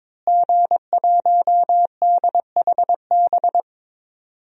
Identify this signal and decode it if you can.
Morse code, heard as a keyed tone, about 22 wpm, 700 Hz. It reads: G1DHB
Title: DXCC-100 Call Signs - 5 Characters - Repeated 3x - 22wpm